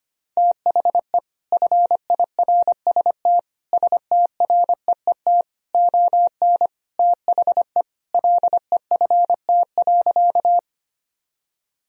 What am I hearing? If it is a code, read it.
Morse code, heard as a keyed tone, about 25 wpm, 700 Hz. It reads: THE FIRST STREET ON THE LEFT.
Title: Sentences from Top 1000 Words - Review Part 3 - 25wpm